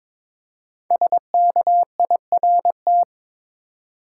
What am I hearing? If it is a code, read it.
Morse code, heard as a keyed tone, about 22 wpm, 700 Hz. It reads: SKIRT
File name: 5 Letter Words - First 3 Letters Given - 22wpm